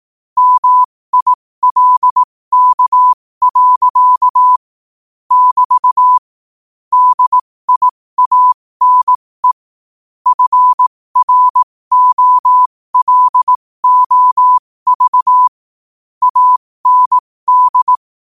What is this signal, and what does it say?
Morse code, heard as a keyed tone, about 18 wpm, spaced as for 15 wpm, 1000 Hz. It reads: MILK. <BT> DIANE FROLOV AND
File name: qotdfiles / 15wpm